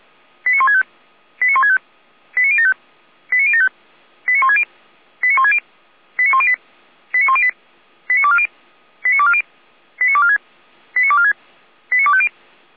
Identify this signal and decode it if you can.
Five-tone selective call (ZVEI), heard as quick runs of five tones, tones of about 80 ms. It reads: 89165 89165 89075 89075 89160 89160 89108 89108 89240 89240 89246 89246 89250
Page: SDR/ZVEI